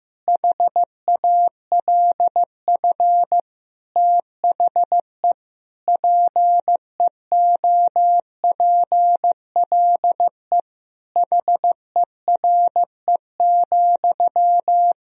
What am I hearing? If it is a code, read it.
Morse code, heard as a keyed tone, about 16 wpm, 700 Hz. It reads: HALF THE PEOPLE HERE,
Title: Sentences from Top 1000 Words - Review Part 1 - 15wpm